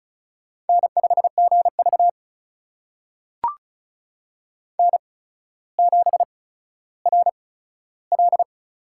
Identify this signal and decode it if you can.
Morse code, heard as a keyed tone, about 36 wpm, spaced as for 15 wpm, 700 Hz. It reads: N5GV N7RL
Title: USA Call Signs - 4 Characters - 8x Character Spacing - 35wpm